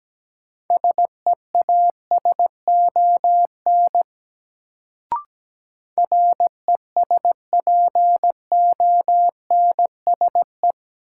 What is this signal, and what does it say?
Morse code, heard as a keyed tone, about 17 wpm, 700 Hz. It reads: SEASON RESPONSE
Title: Sentences from Top 1400 Words - New Word Drill - 17wpm